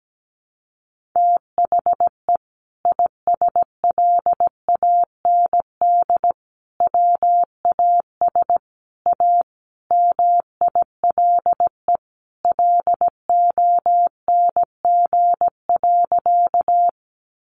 Morse code, heard as a keyed tone, about 17 wpm, 700 Hz. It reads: THE ISLAND WAS A MILE LONG.
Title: Sentences from Top 500 Words 17wpm